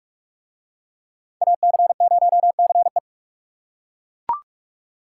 Morse code, heard as a keyed tone, about 38 wpm, 690 Hz. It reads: AC0CE